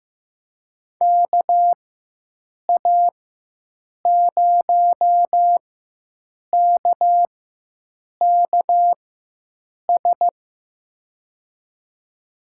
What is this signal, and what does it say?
Morse code, heard as a keyed tone, about 15 wpm, spaced as for 7 wpm, 700 Hz. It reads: KA0KKS
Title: USA Call Signs - 6 Characters - 4x Character Spacing - 15wpm